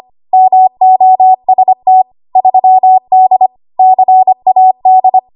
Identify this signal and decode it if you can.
Morse code, heard as a keyed tone, about 25 wpm, 750 Hz. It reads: MOST 3D CAD
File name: Archive / 25 WPM / 25WPM